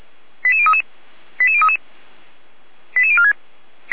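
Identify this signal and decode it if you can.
Five-tone selective call (ZVEI), heard as quick runs of five tones, tones of about 70 ms. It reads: 88033 88033 88046
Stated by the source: ZVEI five-tone call received by SDR.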